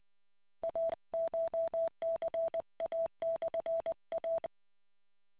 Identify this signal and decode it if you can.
Morse code, heard as a keyed tone, about 20 wpm, 660 Hz. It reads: ACHCA/A